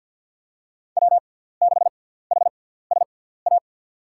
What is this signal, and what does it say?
Morse code, heard as a keyed tone, about 47 wpm, spaced as for 17 wpm, 700 Hz. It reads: W6HSA
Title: USA Call Signs - 5 Characters - 6x Character Spacing - 50wpm